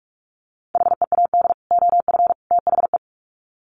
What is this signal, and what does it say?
Morse code, heard as a keyed tone, about 44 wpm, 700 Hz. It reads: HEAD OF THE
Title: Sets of 3 Words 45wpm